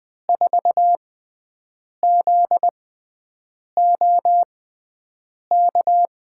Morse code, heard as a keyed tone, about 20 wpm, spaced as for 7 wpm, 700 Hz. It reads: 4ZOK
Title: USA Call Signs - 6 Characters - 6x Character Spacing - 20wpm